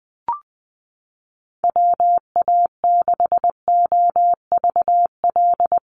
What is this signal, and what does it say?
Morse code, heard as a keyed tone, about 20 wpm, 700 Hz. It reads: WA6OVL
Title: USA Call Signs Repeated 2x 20wpm